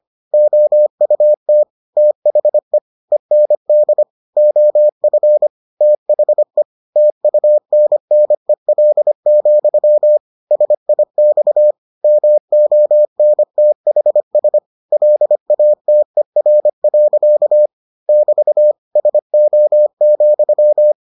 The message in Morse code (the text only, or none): OUT THE END OF THE TUNNEL, SIX MONTHS LATER. <BT> SO,